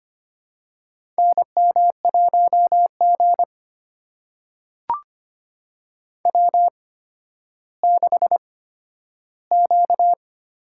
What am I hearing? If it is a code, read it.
Morse code, heard as a keyed tone, about 25 wpm, spaced as for 12 wpm, 700 Hz. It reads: NM1G W6Q
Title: USA Call Signs - 4 Characters - 8x Character Spacing - 25wpm